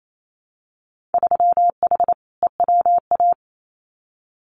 Morse code, heard as a keyed tone, about 28 wpm, 700 Hz. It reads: 3H EWA
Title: Winter Field Day Exchanges Encore 28wpm